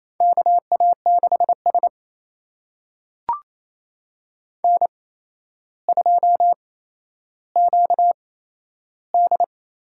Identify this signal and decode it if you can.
Morse code, heard as a keyed tone, about 28 wpm, spaced as for 12 wpm, 700 Hz. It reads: KA6S N2QD
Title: USA Call Signs - 4 Characters - 8x Character Spacing - 28wpm